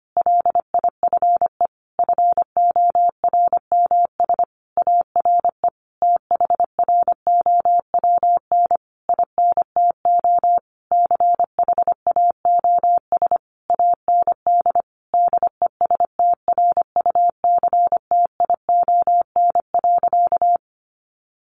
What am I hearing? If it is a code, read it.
Morse code, heard as a keyed tone, about 25 wpm, 700 Hz. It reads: LIFE FORMS ARE THROWN INTO CHAOS AND DESTRUCTION.
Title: A Sound Of Thunder 25wpm